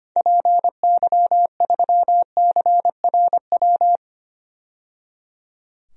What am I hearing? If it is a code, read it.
Morse code, heard as a keyed tone, about 25 wpm, 690 Hz. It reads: PY3CRW
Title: DXCC-100 Call Signs - 6 Characters - 25wpm